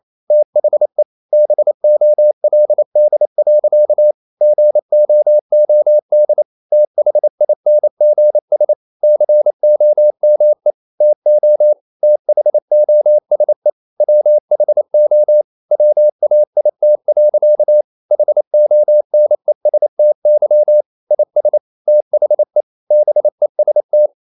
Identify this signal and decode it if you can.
Morse code, heard as a keyed tone, about 28 wpm, 600 Hz. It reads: THE BOLD. GOOD THINGS COME TO THOSE WHO WAIT. HONESTY IS THE BEST
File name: CW / Practice / EnglishSayings28